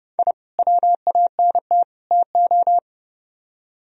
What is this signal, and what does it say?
Morse code, heard as a keyed tone, about 30 wpm, 700 Hz. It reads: I WANT TO